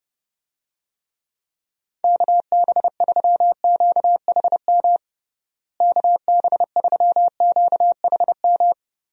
Morse code, heard as a keyed tone, about 30 wpm, 700 Hz. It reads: KB3QHM KB3QHM
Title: USA Call Signs - 6 Characters - Repeated 3x - 30wpm